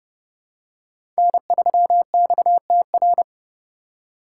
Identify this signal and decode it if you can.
Morse code, heard as a keyed tone, about 30 wpm, 700 Hz. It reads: N3XTR
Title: USA Call Signs - 5 Characters - 4x Character Spacing - 30wpm